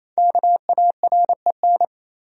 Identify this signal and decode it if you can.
Morse code, heard as a keyed tone, about 28 wpm, 700 Hz. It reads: KAREN
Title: USA Common Names - Max 5 Letters - 28wpm